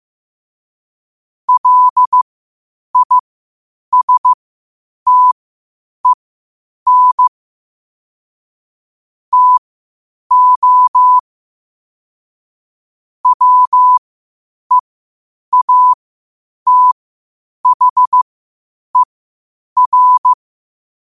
Morse code, heard as a keyed tone, about 15 wpm, spaced as for 8 wpm, 1000 Hz. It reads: LISTEN TO WEATHER
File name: qotdfiles / 8wpm